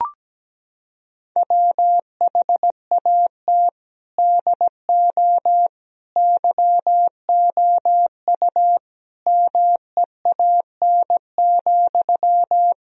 Morse code, heard as a keyed tone, about 17 wpm, 700 Hz. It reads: WHAT DO YOU MEAN,
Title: Sentences from Top 1000 Words - Review Part 4 - 17wpm